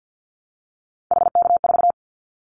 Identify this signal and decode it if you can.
Morse code, heard as a keyed tone, about 50 wpm, 700 Hz. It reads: HK4
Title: Callsign Prefixes for Top 100 DXCC - 50wpm